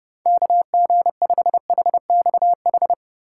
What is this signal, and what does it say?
Morse code, heard as a keyed tone, about 30 wpm, 700 Hz. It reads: KG5HXH
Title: USA Call Signs - 8x Character Spacing - 30wpm